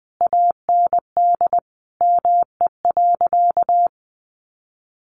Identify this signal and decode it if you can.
Morse code, heard as a keyed tone, about 20 wpm, 700 Hz. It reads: AND ME.
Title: Sentences from Top 500 Words 20wpm